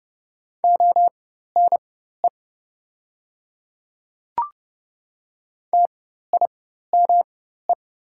Morse code, heard as a keyed tone, about 30 wpm, spaced as for 14 wpm, 700 Hz. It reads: ONE TIME
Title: 4x Character Spacing - Top 100 Words - 30wpm